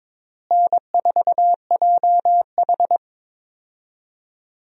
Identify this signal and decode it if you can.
Morse code, heard as a keyed tone, about 22 wpm, 700 Hz. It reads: N4JH
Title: USA Call Signs - 4 Characters - Repeated 3x - 22wpm